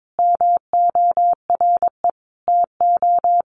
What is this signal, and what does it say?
Morse code, heard as a keyed tone, about 22 wpm, 700 Hz. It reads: MORE TO
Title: Sentences from Top 100 Words 22wpm